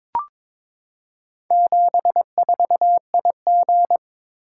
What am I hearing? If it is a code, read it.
Morse code, heard as a keyed tone, about 22 wpm, 700 Hz. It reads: 74IG
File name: practice / 22wpm / Four Letter-Number Combinations - 22wpm